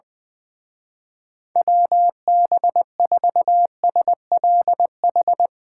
Morse code, heard as a keyed tone, about 22 wpm, 700 Hz. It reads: WB4SLH
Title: Speed Racing - USA 2x3 Call Signs 20wpm